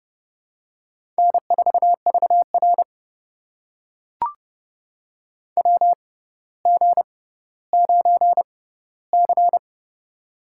Morse code, heard as a keyed tone, about 30 wpm, spaced as for 15 wpm, 700 Hz. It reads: N4VR WG9C
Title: USA Call Signs - 4 Characters - 6x Character Spacing - 30wpm